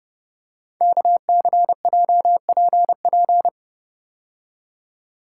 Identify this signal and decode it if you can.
Morse code, heard as a keyed tone, about 30 wpm, 700 Hz. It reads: KCJPP
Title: Code Groups - 5 Letters - 30wpm